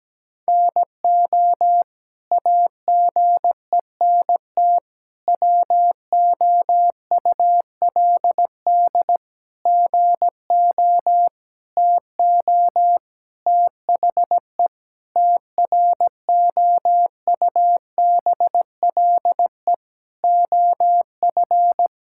Morse code, heard as a keyed tone, about 17 wpm, 700 Hz. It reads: NO AGENT WOULD GO TO THE TROUBLE OF
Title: Sentences from Top 1600 Words 17wpm